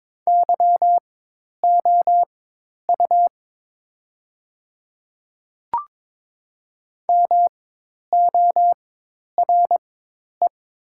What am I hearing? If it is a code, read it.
Morse code, heard as a keyed tone, about 21 wpm, spaced as for 10 wpm, 700 Hz. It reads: YOU MORE
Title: 4x Character Spacing - Top 100 Words - 22wpm